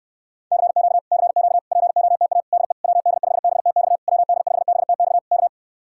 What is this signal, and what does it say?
Morse code, heard as a keyed tone, about 69 wpm, 700 Hz. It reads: CQ CQ POTA DE WD5BTX WD5BTX K